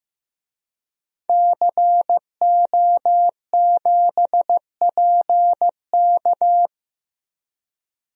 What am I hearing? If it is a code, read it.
Morse code, heard as a keyed tone, about 15 wpm, 700 Hz. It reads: CO7PK
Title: DXCC-100 Call Signs - 5 Characters - Repeated 2x - 15wpm